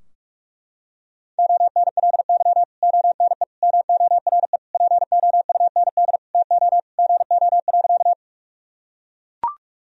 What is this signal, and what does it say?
Morse code, heard as a keyed tone, about 41 wpm, 700 Hz. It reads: ONLY ONE MORE POUND TO GO.
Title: Sentences from Top 1000 Words - Review Part 2 - 45wpm